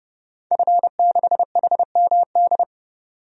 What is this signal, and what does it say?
Morse code, heard as a keyed tone, about 30 wpm, 700 Hz. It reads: F6HMD
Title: DXCC-100 Call Signs - 5 Characters - 30wpm